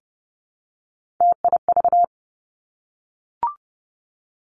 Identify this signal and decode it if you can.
Morse code, heard as a keyed tone, about 30 wpm, 700 Hz. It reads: TIV
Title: Three Challenging Letter-Number Combinations 30wpm